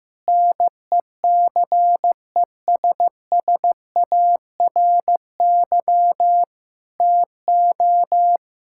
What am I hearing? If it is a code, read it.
Morse code, heard as a keyed tone, about 15 wpm, 700 Hz. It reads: NECESSARY TO
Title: Sentences from Top 1600 Words 15wpm